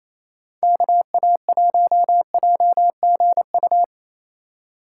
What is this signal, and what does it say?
Morse code, heard as a keyed tone, about 28 wpm, 700 Hz. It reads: KA1JGU
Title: Speed Racing - USA 2x3 Call Signs 28wpm